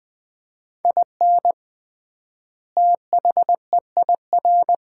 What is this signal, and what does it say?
Morse code, heard as a keyed tone, about 21 wpm, 700 Hz. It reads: IN THEIR
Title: 3x Word-Spacing - Sets of 2 Words - 20wpm